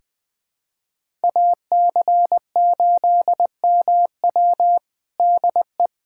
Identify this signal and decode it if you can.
Morse code, heard as a keyed tone, about 20 wpm, 700 Hz. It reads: AC8MW DE
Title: USA Callsign DE Callsign - 5 Characters - 20wpm